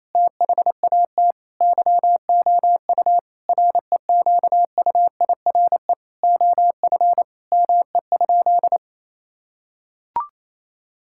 Morse code, heard as a keyed tone, about 28 wpm, 700 Hz. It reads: THAT YOU REQUIRE OF ME?
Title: Sentences from Top 1000 Words - Review Part 6 - 28wpm